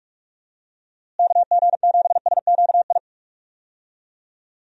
Morse code, heard as a keyed tone, about 44 wpm, 700 Hz. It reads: KG7SQI
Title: USA Call Signs - 6 Characters - Repeated 3x - 45wpm